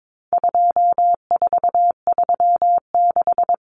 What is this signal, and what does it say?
Morse code, heard as a keyed tone, about 22 wpm, 700 Hz. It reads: 2436
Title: Numbers - 4-Digit 22wpm